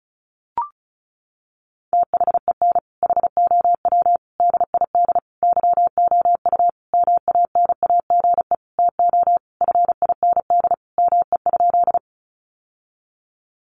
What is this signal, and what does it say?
Morse code, heard as a keyed tone, about 34 wpm, 700 Hz. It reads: THEN HOW DID YOU MANAGE TO FIND ME?
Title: Sentences from Top 1400 Words 35wpm